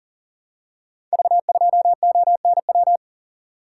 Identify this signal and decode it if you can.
Morse code, heard as a keyed tone, about 42 wpm, 690 Hz. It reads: V2ONW
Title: Code Groups - 5 Letter:Numbers - 40wpm